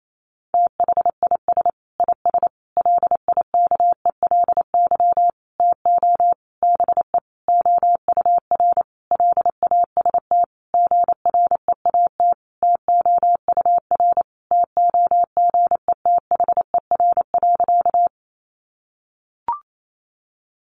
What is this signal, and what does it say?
Morse code, heard as a keyed tone, about 28 wpm, 700 Hz. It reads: THIS IS LIKELY TO BE OUR LAST GREAT TOUR TOGETHER.
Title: Sentences from Top 1400 Words 28wpm